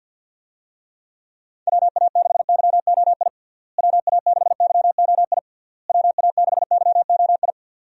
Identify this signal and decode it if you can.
Morse code, heard as a keyed tone, about 49 wpm, 700 Hz. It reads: WA6YOI WA6YOI WA6YOI
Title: USA Call Signs - 6 Characters - Repeated 3x - 50wpm